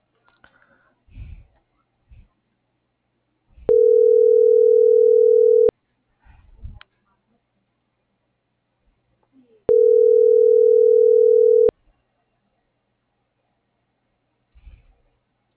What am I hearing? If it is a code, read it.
Ringback tone (North America), 440 + 480 Hz.